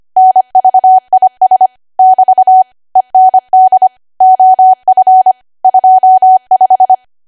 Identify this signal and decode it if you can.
Morse code, heard as a keyed tone, about 25 wpm, 750 Hz. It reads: NVIS <BT> END OF 25
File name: Archive / 25 WPM / 25WPM